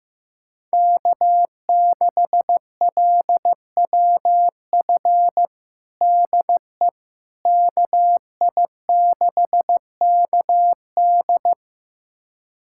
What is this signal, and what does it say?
Morse code, heard as a keyed tone, about 16 wpm, 700 Hz. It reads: K6LWF DE KI6KD
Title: USA Callsign DE Callsign - 5 Characters - 15wpm